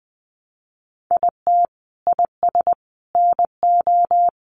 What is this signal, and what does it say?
Morse code, heard as a keyed tone, about 20 wpm, 700 Hz. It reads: IT IS NO
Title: Sentences from Top 1200 Words 20wpm